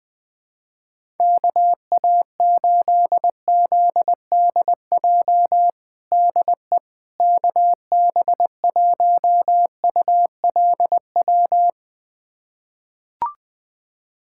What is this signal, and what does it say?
Morse code, heard as a keyed tone, about 20 wpm, 700 Hz. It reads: KA8ZDJ DE KB1ULW
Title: USA Callsign DE Callsign 20wpm